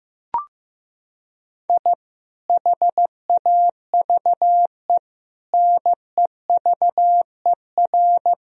Morse code, heard as a keyed tone, about 15 wpm, 700 Hz. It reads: I HAVE NEVER
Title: Sentences from Top 1000 Words - Review Part 1 - 15wpm